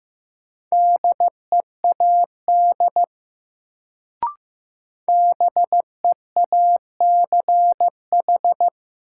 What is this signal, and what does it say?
Morse code, heard as a keyed tone, about 15 wpm, 700 Hz. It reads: DEAD BEACH
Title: Sentences from Top 1500 Words - New Word Drill - 15wpm